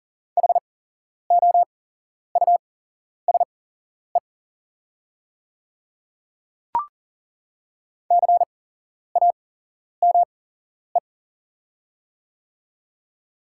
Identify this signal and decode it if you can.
Morse code, heard as a keyed tone, about 38 wpm, spaced as for 11 wpm, 700 Hz. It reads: HOUSE CAME
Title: 8x Character Spacing - Top 200 Words - 40wpm